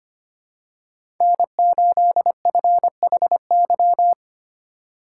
Morse code, heard as a keyed tone, about 25 wpm, 700 Hz. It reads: N8FHY
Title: USA Call Signs - 5 Characters - 3x Character Spacing - 25wpm